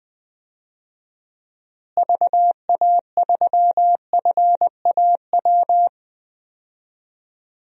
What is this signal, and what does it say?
Morse code, heard as a keyed tone, about 20 wpm, 700 Hz. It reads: VA3FAW
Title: DXCC-100 Call Signs - 6 Characters - Repeated 3x - 20wpm